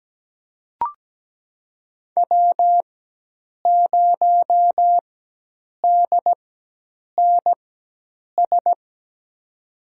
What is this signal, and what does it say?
Morse code, heard as a keyed tone, about 17 wpm, spaced as for 8 wpm, 700 Hz. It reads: W0DNS
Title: USA Call Signs - 5 Characters - 4x Character Spacing - 17wpm